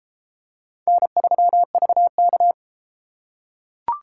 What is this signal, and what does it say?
Morse code, heard as a keyed tone, about 30 wpm, 700 Hz. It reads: N3VK